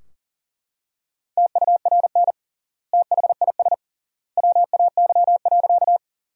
Morse code, heard as a keyed tone, about 37 wpm, 700 Hz. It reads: TURN THIS WAY.
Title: 3x Word-Spacing - Sentences from Top 200 Words - 40wpm